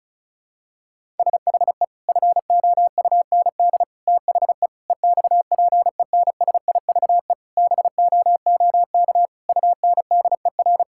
Morse code, heard as a keyed tone, about 36 wpm, 700 Hz. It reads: SHE FOUND THE EXPENSIVE BOOK UNDER
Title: Sentences from Top 1000 Words - Review Part 6 - 35wpm